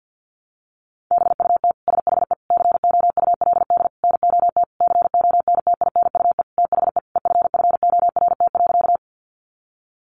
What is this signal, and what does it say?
Morse code, heard as a keyed tone, about 50 wpm, 700 Hz. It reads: BUT SHE COULD NOT CONTINUE THE EFFORT.
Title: Sentences from Top 1200 Words 50wpm